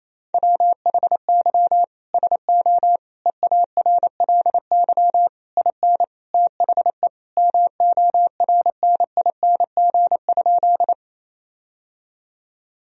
Morse code, heard as a keyed tone, about 28 wpm, 700 Hz. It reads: WHY SO EARLY IN THE MORNING?
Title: Sentences from Top 1000 Words - Review Part 2 - 28wpm